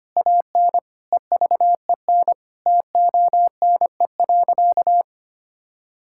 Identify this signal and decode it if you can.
Morse code, heard as a keyed tone, about 25 wpm, 700 Hz. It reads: AN EVEN TONE.
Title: Sentences from Top 1600 Words 25wpm